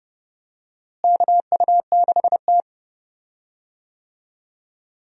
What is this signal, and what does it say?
Morse code, heard as a keyed tone, about 30 wpm, 700 Hz. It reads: KU6T
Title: USA Call Signs - 4 Characters - Repeated 2x - 30wpm